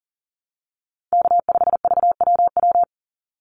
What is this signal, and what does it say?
Morse code, heard as a keyed tone, about 37 wpm, 700 Hz. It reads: K5VWW
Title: Call Signs Repeated 2x - 40wpm